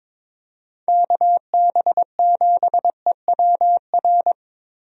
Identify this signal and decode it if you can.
Morse code, heard as a keyed tone, about 22 wpm, 700 Hz. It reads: KB7EWR
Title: USA Call Signs - 6 Characters - Repeated 2x - 22wpm